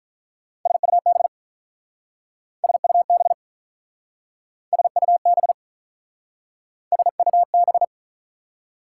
Morse code, heard as a keyed tone, about 46 wpm, 690 Hz. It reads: SUB SUB SUB SUB